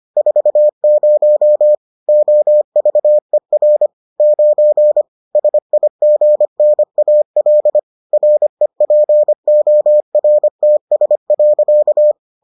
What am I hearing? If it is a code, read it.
Morse code, heard as a keyed tone, about 25 wpm, 600 Hz. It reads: 40 OVER 9 SIGNAL REPORTS.